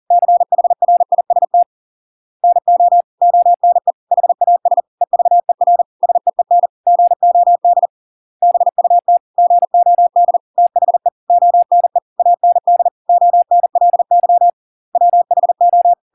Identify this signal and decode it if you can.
Morse code, heard as a keyed tone, about 40 wpm, 700 Hz. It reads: CHRIST NO ONE HAS EVER SEEN GOD BUT GOD THE ONE AND ONLY WHO